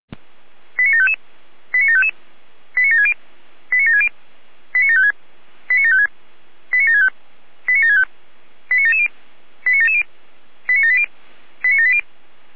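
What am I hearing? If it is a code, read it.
Five-tone selective call (ZVEI), heard as quick runs of five tones, tones of about 70 ms. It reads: 89755 89755 89760 89760 89756 89756 89765 89765 89770 89770 89780 89780
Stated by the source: Five-tone ZVEI selective call 89750 received by SDR.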